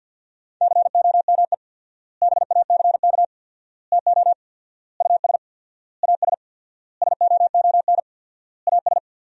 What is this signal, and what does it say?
Morse code, heard as a keyed tone, about 50 wpm, 700 Hz. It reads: COME BACK TO US AS SOON AS
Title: 4x Word-Spacing - Sentences from Top 500 Words - 50wpm